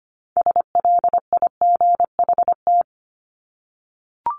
Morse code, heard as a keyed tone, about 25 wpm, 700 Hz. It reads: SLIGHT
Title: LY Suffix - Top 100 Words with Base Words 25wpm